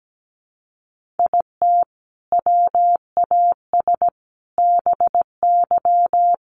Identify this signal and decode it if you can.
Morse code, heard as a keyed tone, about 17 wpm, 700 Hz. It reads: IT WAS BY
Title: Sentences from Top 1500 Words 17wpm